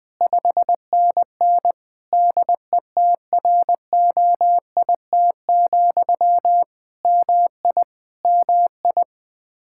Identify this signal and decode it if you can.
Morse code, heard as a keyed tone, about 20 wpm, 700 Hz. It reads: 5NN DETROIT, MI MI
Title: Practice QSO - Exchange 1 - Beginning - 20wpm